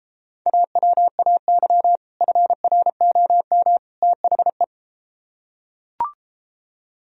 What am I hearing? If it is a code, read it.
Morse code, heard as a keyed tone, about 33 wpm, 700 Hz. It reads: AWAY FROM THE